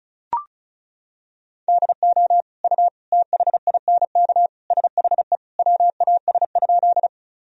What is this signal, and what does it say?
Morse code, heard as a keyed tone, about 34 wpm, 700 Hz. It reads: DO U THINK SHE WAS?
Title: Sentences from Top 200 Words and Abbreviations 35wpm